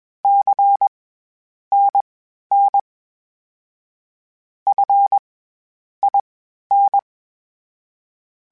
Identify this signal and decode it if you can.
Morse code, heard as a keyed tone, about 20 wpm, spaced as for 10 wpm, 800 Hz. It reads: CNN FIN
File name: CW Comparisons / CW Comparisons-mp3s / CW Comparisons Broken Characters @20